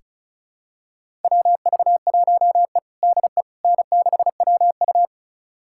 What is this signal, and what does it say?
Morse code, heard as a keyed tone, about 36 wpm, 700 Hz. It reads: WV1E DE N6WU
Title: USA Callsign DE Callsign - 4 Characters - 35wpm